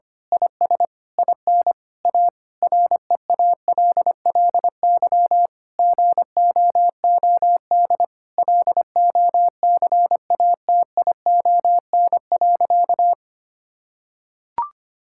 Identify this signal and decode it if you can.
Morse code, heard as a keyed tone, about 25 wpm, 700 Hz. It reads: IS IN A REALLY GOOD LOCATION.
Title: Sentences from Top 1000 Words - Review Part 6 - 25wpm